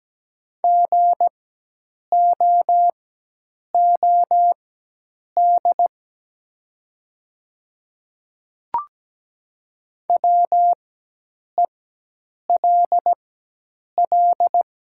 Morse code, heard as a keyed tone, about 17 wpm, spaced as for 8 wpm, 700 Hz. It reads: GOOD WELL
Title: 4x Character Spacing - Top 200 Words - 17wpm